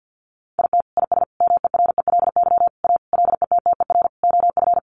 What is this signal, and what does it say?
Morse code, heard as a keyed tone, about 50 wpm, 700 Hz. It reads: IT IS MERELY A LETTER OF
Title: Sentences from Top 1400 Words 50wpm